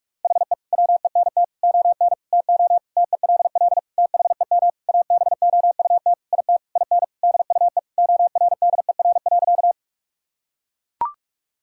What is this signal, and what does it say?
Morse code, heard as a keyed tone, about 41 wpm, 700 Hz. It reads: HE WENT ON TO TELL THEM ABOUT IT IN DUE ORDER.
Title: Sentences from Top 1000 Words - Review Part 6 - 45wpm